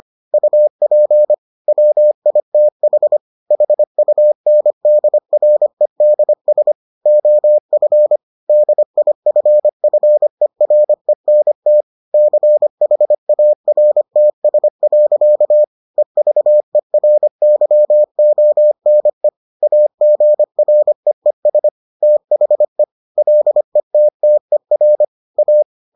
Morse code, heard as a keyed tone, about 25 wpm, 600 Hz. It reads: UP WITH HUNDREDS OF DIFFERENT CHARTS. EVERYONE AGREES THE LETTER A